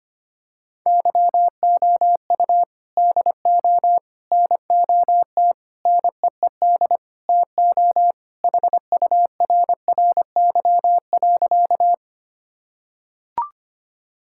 Morse code, heard as a keyed tone, about 25 wpm, 700 Hz. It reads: YOU DO NOT NEED TO HURRY.
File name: practice / 25wpm / Sentences with 4-Word Phrases - 25wpm